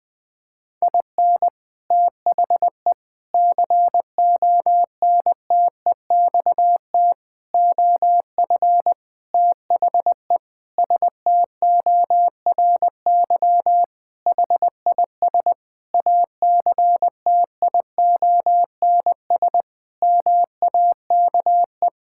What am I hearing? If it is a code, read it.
Morse code, heard as a keyed tone, about 20 wpm, 700 Hz. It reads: IN THE CONTEXT OF THE STORY HIS ACTIONS MAKE
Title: Sentences with 5-Word Phrases - 20wpm